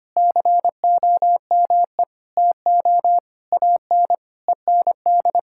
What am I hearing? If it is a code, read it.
Morse code, heard as a keyed tone, about 25 wpm, 700 Hz. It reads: COME TO AN END
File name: Sets of 9 Words - 25wpm